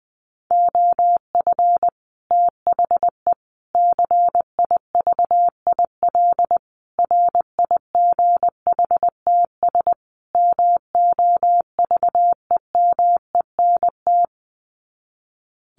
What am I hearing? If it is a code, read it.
Morse code, heard as a keyed tone, about 20 wpm, 700 Hz. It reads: OF THE CIVIL RIGHTS MOVEMENT
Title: Sets of 5 Words Encore2 20wpm